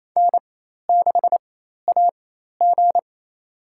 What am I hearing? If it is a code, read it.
Morse code, heard as a keyed tone, about 28 wpm, spaced as for 13 wpm, 700 Hz. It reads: N6AG
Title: USA Call Signs - 4 Characters - 4x Character Spacing - 28wpm